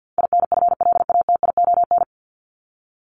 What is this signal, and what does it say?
Morse code, heard as a keyed tone, about 50 wpm, 700 Hz. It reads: INFLATION
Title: Top 1001-1250 Words 50wpm